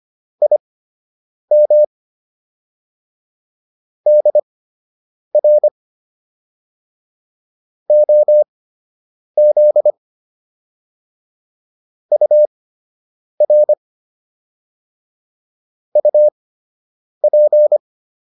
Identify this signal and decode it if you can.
Morse code, heard as a keyed tone, about 25 wpm, spaced as for 8 wpm, 600 Hz. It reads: IM DR OZ UR UP